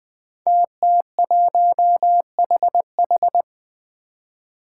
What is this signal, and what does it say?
Morse code, heard as a keyed tone, about 20 wpm, 700 Hz. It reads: TT1HH
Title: Five Challenging Letter-Number Combinations 20wpm